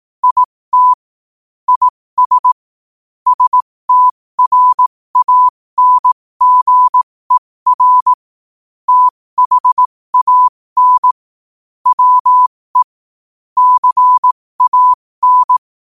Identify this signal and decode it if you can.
Morse code, heard as a keyed tone, about 18 wpm, spaced as for 15 wpm, 1000 Hz. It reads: IT IS STRANGER THAN WE CAN
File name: qotdfiles / 15wpm